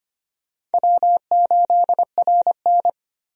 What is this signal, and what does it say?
Morse code, heard as a keyed tone, about 25 wpm, 700 Hz. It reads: W8RN
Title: USA Call Signs - 4 Characters - 25wpm